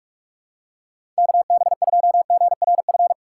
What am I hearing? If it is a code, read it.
Morse code, heard as a keyed tone, about 41 wpm, 700 Hz. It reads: KB2GRF
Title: USA Call Signs - 6 Characters - 3x Character Spacing - 45wpm